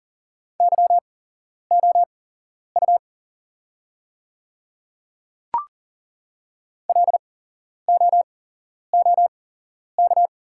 Morse code, heard as a keyed tone, about 40 wpm, spaced as for 11 wpm, 700 Hz. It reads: YOU LOOK
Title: 8x Character Spacing - Top 200 Words - 40wpm